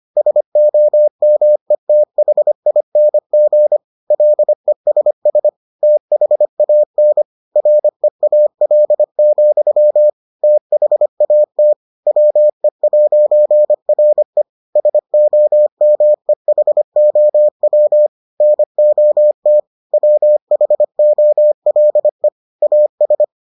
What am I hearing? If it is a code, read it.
Morse code, heard as a keyed tone, about 25 wpm, 600 Hz. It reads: SOMETHING LESS THAN REAL, THAT WE'RE SOMEHOW NOT WHOLE AS